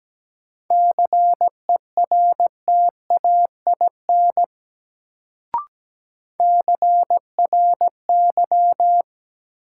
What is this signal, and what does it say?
Morse code, heard as a keyed tone, about 17 wpm, 700 Hz. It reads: CERTAIN CRY
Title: Sentences from Top 500 Words - New Word Drill - 17wpm